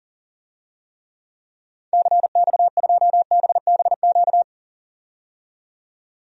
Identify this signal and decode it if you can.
Morse code, heard as a keyed tone, about 37 wpm, 700 Hz. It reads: CX2BBQ
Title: DXCC-100 Call Signs - 6 Characters - 40wpm